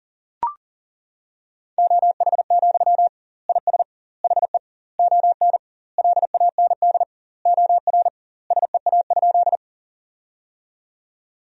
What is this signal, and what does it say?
Morse code, heard as a keyed tone, about 40 wpm, 700 Hz. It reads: OH, IS HE ON LAND OR SEA?
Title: 2x Word-Spacing - Sentences from Top 500 Words - 40wpm